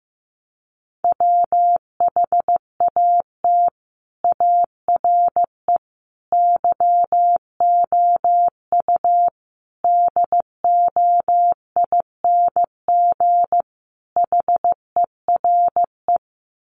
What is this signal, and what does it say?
Morse code, heard as a keyed tone, about 15 wpm, 700 Hz. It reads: WHAT ARE YOU DOING HERE
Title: Sets of 5 Words 15wpm